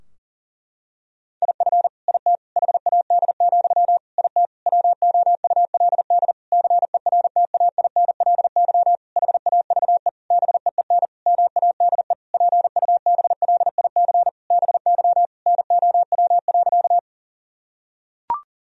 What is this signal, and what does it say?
Morse code, heard as a keyed tone, about 40 wpm, 700 Hz. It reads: IF IT HAD, IT WOULD CERTAINLY HAVE BEEN MADE PUBLIC BY NOW.
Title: Sentences from Top 1000 Words - Review Part 3 - 40wpm